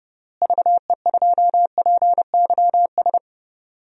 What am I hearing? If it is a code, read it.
Morse code, heard as a keyed tone, about 32 wpm, 700 Hz. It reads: VE2PYS
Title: DXCC-100 Call Signs - 6 Characters - Repeated 3x - 30wpm